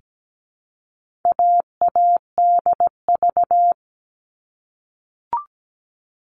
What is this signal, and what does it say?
Morse code, heard as a keyed tone, about 17 wpm, 700 Hz. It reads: AADV
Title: All Cut Numbers - 4-Digit 17wpm